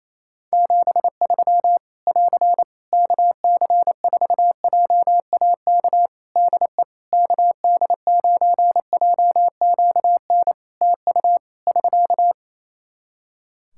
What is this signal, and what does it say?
Morse code, heard as a keyed tone, about 28 wpm, 700 Hz. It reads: 73 <AR> KC4JAK DE KD9JQN TU <SK>
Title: Practice QSO - Exchange 4 - Ending - 28wpm